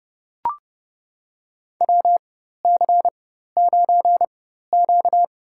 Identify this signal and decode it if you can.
Morse code, heard as a keyed tone, about 30 wpm, spaced as for 14 wpm, 700 Hz. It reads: WC9Q
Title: USA Call Signs - 4 Characters - 4x Character Spacing - 30wpm